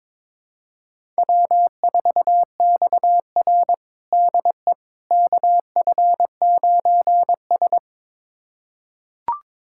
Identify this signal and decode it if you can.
Morse code, heard as a keyed tone, about 22 wpm, 700 Hz. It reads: W4XR DE KF9S
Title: USA Callsign DE Callsign - 4 Characters - 22wpm